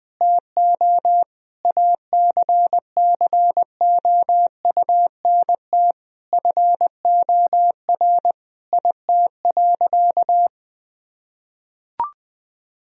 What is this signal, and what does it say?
Morse code, heard as a keyed tone, about 21 wpm, 700 Hz. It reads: TO ACCOUNT FOR IT.
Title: Sentences from Top 1000 Words - Review Part 2 - 20wpm